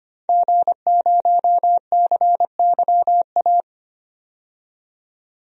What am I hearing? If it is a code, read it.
Morse code, heard as a keyed tone, about 25 wpm, 700 Hz. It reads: G0CYA
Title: DXCC-100 Call Signs - 5 Characters - 25wpm